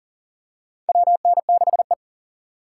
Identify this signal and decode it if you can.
Morse code, heard as a keyed tone, about 38 wpm, 700 Hz. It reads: WN6E
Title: USA Call Signs - 4 Characters - 2x Character Spacing - 40wpm